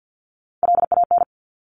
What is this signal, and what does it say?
Morse code, heard as a keyed tone, about 50 wpm, 700 Hz. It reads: LAN
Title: Top Three Letter Combinations in Words 50wpm